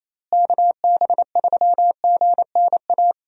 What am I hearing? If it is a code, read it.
Morse code, heard as a keyed tone, about 28 wpm, 700 Hz. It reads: KB3GNA
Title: USA Call Signs - 6 Characters - Repeated 2x - 28wpm